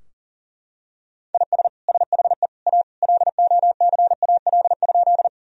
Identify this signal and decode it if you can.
Morse code, heard as a keyed tone, about 40 wpm, 700 Hz. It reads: IS SHE A LOCAL?
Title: Sentences from Top 1000 Words - Review Part 3 - 40wpm